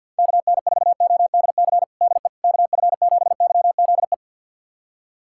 Morse code, heard as a keyed tone, about 45 wpm, 700 Hz. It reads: KN4ODC DE KF7YZE
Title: USA Callsign DE Callsign 50wpm